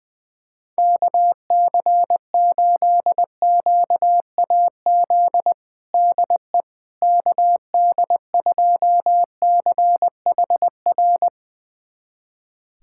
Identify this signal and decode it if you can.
Morse code, heard as a keyed tone, about 21 wpm, 700 Hz. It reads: KC8QAZ DE KD2CHR
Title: USA Callsign DE Callsign - 6 Characters - 20wpm